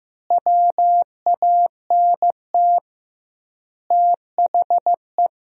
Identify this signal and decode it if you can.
Morse code, heard as a keyed tone, about 15 wpm, 700 Hz. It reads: WANT THE
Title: 2x Word-Spacing - Sentences from Top 400 Words - 15wpm